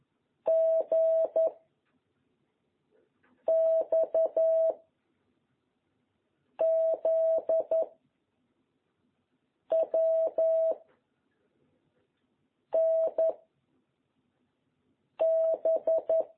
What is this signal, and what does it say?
Morse code, heard as a keyed tone, about 11 wpm, 660 Hz. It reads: G X Z W N B
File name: jh8cbh / 2021cw